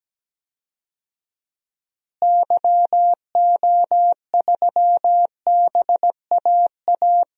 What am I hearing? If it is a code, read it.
Morse code, heard as a keyed tone, about 17 wpm, 700 Hz. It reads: YO3BAA
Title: DXCC-100 Call Signs - 6 Characters - Repeated 3x - 17wpm